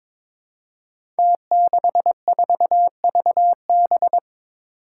Morse code, heard as a keyed tone, about 21 wpm, 700 Hz. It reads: T64VB
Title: Five Challenging Letter-Number Combinations 22wpm